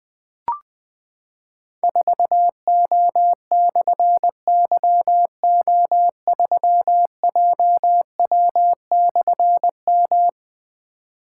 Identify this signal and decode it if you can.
Morse code, heard as a keyed tone, about 20 wpm, 700 Hz. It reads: 4O/YO3JW/M
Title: Difficult Call Signs 20wpm